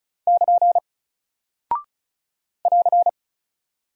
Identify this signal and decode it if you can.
Morse code, heard as a keyed tone, about 34 wpm, 700 Hz. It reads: <KN> <AR>
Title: Single Character (Letter:Number:Punctuation:Prosign) Alt 35wpm